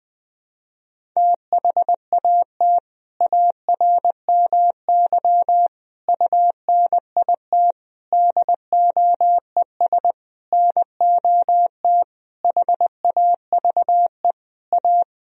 Morse code, heard as a keyed tone, about 20 wpm, 700 Hz. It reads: THAT ARMY UNIT DOES NOT HAVE A
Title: Sentences from Top 1000 Words - Review Part 2 - 20wpm